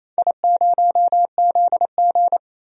Morse code, heard as a keyed tone, about 28 wpm, 700 Hz. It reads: I0ZG
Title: DXCC-100 Call Signs - 4 Characters - 28wpm